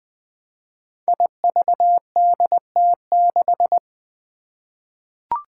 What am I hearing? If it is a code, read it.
Morse code, heard as a keyed tone, about 20 wpm, 700 Hz. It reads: IVDT6
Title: Five Challenging Letter-Number Combinations 20wpm